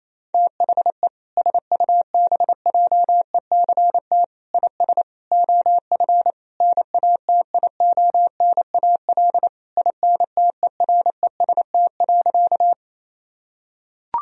Morse code, heard as a keyed tone, about 27 wpm, 700 Hz. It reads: THE SUBJECT IS OF NATIONAL INTEREST.
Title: Sentences from Top 1000 Words - Review Part 4 - 28wpm